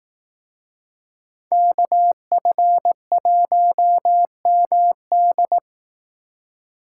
Morse code, heard as a keyed tone, about 18 wpm, 700 Hz. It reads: KF1MD
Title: Speed Racing - USA 2x2 Call Signs 15wpm